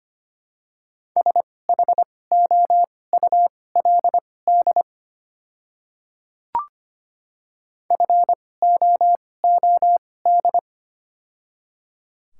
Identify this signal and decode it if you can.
Morse code, heard as a keyed tone, about 25 wpm, spaced as for 18 wpm, 700 Hz. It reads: SHOULD FOOD
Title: 2x Character Spacing - Top 300 Words - 25wpm